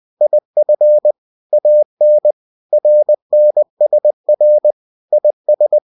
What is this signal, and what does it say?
Morse code, heard as a keyed tone, about 20 wpm, 600 Hz. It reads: IF AN RNSR IS
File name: cw-books / winnie-the-pooh / winnie-20wpm-page-0029